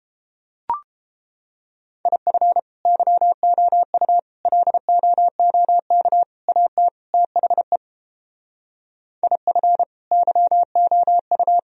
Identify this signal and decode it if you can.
Morse code, heard as a keyed tone, about 32 wpm, 700 Hz. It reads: IF YOU LOOK AT THE IF YOU